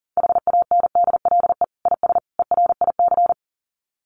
Morse code, heard as a keyed tone, about 40 wpm, 700 Hz. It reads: HANDLE IS ERIC
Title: Common QSO Phrases 40wpm